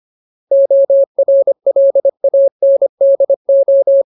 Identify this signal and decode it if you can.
Morse code, heard as a keyed tone, about 25 wpm, 550 Hz. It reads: ORLANDO
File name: Ultimate Word List / 25WPM Ultimate